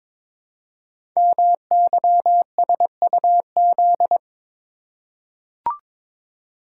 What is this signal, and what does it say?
Morse code, heard as a keyed tone, about 22 wpm, 700 Hz. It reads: MYSUZ
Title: Code Groups - 5 Letters - 22wpm